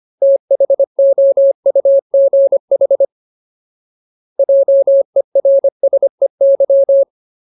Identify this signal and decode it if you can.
Morse code, heard as a keyed tone, about 25 wpm, 550 Hz. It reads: THOUGH JERSEY
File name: Ultimate Word List / 25WPM Ultimate